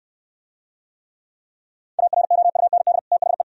Morse code, heard as a keyed tone, about 69 wpm, 700 Hz. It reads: AROUND THE